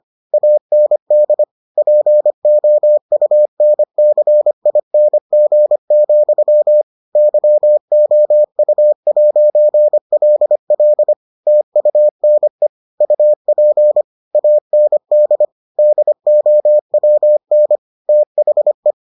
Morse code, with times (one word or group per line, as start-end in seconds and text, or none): 0.3-1.4 AND
1.8-6.8 POUNCING,
7.2-11.1 YOU'LL
11.5-12.7 TUNE
13.0-14.0 UP
14.3-15.4 AND
15.8-17.8 DOWN
18.1-18.9 THE